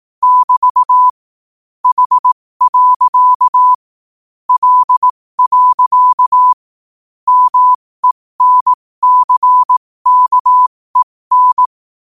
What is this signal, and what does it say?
Morse code, heard as a keyed tone, about 18 wpm, spaced as for 15 wpm, 1000 Hz. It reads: <BT> H. L. MENCKEN
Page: New quotes every day in morse code at 15 Words per minute.